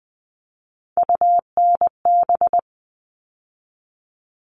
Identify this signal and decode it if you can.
Morse code, heard as a keyed tone, about 20 wpm, 700 Hz. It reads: UNB
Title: All Cut Numbers - 3-Digit 20wpm